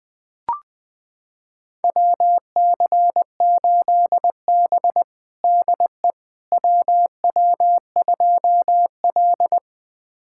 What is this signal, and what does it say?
Morse code, heard as a keyed tone, about 21 wpm, 700 Hz. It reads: WC8B DE WW2L
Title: USA Callsign DE Callsign - 4 Characters - 20wpm